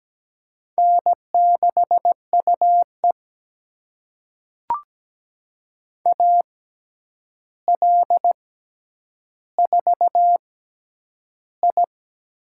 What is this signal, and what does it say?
Morse code, heard as a keyed tone, about 17 wpm, spaced as for 9 wpm, 700 Hz. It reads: N6UE AL4I
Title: USA Call Signs - 4 Characters - 6x Character Spacing - 17wpm